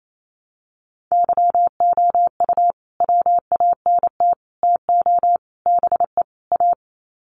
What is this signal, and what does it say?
Morse code, heard as a keyed tone, about 28 wpm, 700 Hz. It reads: YOU WANT TO BE A
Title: Sets of 5 Words Encore2 28wpm